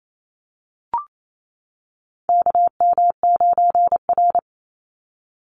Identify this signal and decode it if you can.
Morse code, heard as a keyed tone, about 28 wpm, 700 Hz. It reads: KM9R
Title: Call Signs - 28wpm